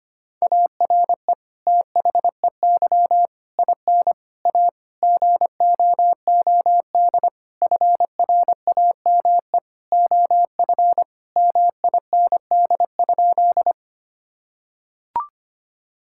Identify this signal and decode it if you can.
Morse code, heard as a keyed tone, about 25 wpm, 700 Hz. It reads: ARE THEY IN A GOOD FRAME OF MIND?
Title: Sentences from Top 1000 Words - Review Part 6 - 25wpm